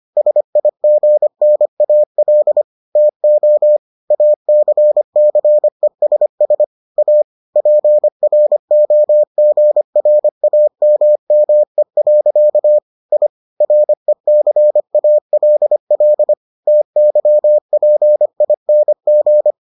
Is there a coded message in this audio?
Yes: Morse code from 0.2 to 19.5 s, heard as a keyed tone, about 25 wpm, 600 Hz. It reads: SIGNAL TO ACCESS A PROGRAMME. I RECALL TYPING